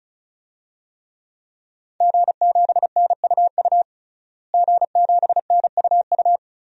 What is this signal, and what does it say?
Morse code, heard as a keyed tone, about 36 wpm, 700 Hz. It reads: G7NUU G7NUU
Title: DXCC-100 Call Signs - 5 Characters - Repeated 2x - 35wpm